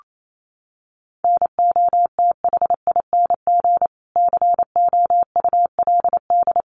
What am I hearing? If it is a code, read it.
Morse code, heard as a keyed tone, about 28 wpm, 700 Hz. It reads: NOTHING COULD
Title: Sentences from Top 500 Words 28wpm